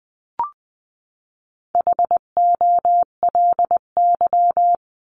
Morse code, heard as a keyed tone, about 20 wpm, 700 Hz. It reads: HOLY
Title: 259 Lesson 42d - Words 18-20wpm